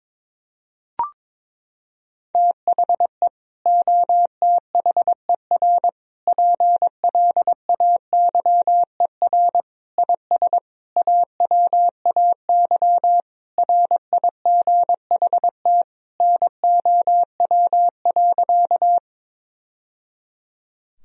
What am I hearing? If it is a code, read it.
Morse code, heard as a keyed tone, about 22 wpm, 700 Hz. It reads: THE OTHER PLAYER IS AWAY RIGHT NOW.
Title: Sentences from Top 1000 Words - Review Part 4 - 22wpm